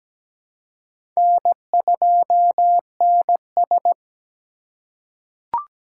Morse code, heard as a keyed tone, about 17 wpm, 700 Hz. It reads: N2NS
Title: K1USN SST QSO Practice - 17wpm